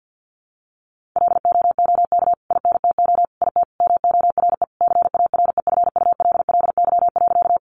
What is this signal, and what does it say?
Morse code, heard as a keyed tone, about 48 wpm, 700 Hz. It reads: LOOK INTO IT MORE CAREFULLY.
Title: Sentences from Top 1300 Words 50wpm